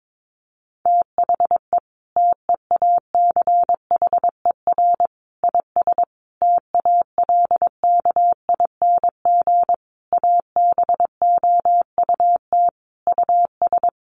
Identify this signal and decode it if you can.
Morse code, heard as a keyed tone, about 22 wpm, 700 Hz. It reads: THE TEACHER IS TALKING ABOUT US
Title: Easy Phrases 22wpm